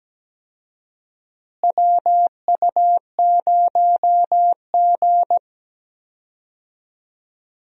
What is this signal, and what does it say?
Morse code, heard as a keyed tone, about 17 wpm, 700 Hz. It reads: WU0G
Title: USA Call Signs - 4 Characters - Repeated 2x - 17wpm